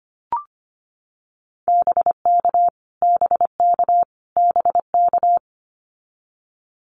Morse code, heard as a keyed tone, about 25 wpm, 700 Hz. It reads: BK BK BK
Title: Abbreviations Repeated 3x - 25wpm